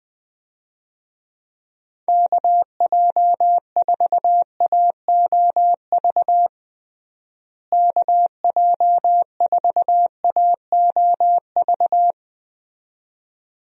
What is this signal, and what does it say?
Morse code, heard as a keyed tone, about 20 wpm, 700 Hz. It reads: KJ4AOV KJ4AOV
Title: USA Call Signs - 6 Characters - Repeated 2x - 20wpm